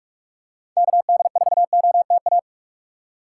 Morse code, heard as a keyed tone, about 44 wpm, 700 Hz. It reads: KD4OTA